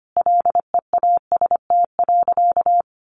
Morse code, heard as a keyed tone, about 25 wpm, 700 Hz. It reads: LEAST.
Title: Sentences from Top 1300 Words 25wpm